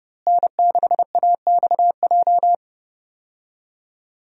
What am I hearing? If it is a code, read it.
Morse code, heard as a keyed tone, about 32 wpm, 700 Hz. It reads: N6AXJ
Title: USA Call Signs - 5 Characters - Repeated 2x - 30wpm